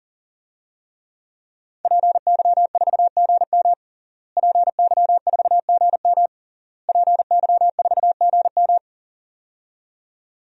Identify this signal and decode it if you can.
Morse code, heard as a keyed tone, about 40 wpm, 700 Hz. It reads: PY4GM PY4GM PY4GM
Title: DXCC-100 Call Signs - 5 Characters - Repeated 3x - 40wpm